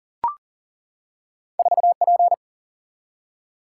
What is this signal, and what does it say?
Morse code, heard as a keyed tone, about 40 wpm, 700 Hz. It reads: 4P
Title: Single-Letter Single-Number 40wpm